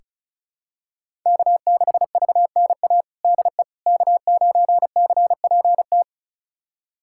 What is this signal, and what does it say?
Morse code, heard as a keyed tone, about 34 wpm, 700 Hz. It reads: K6VNA DE K9CPT
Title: USA Callsign DE Callsign - 5 Characters - 35wpm